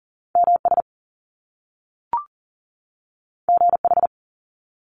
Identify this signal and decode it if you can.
Morse code, heard as a keyed tone, about 42 wpm, 700 Hz. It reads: MS GH
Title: Top Two Letter Combinations in Words 40wpm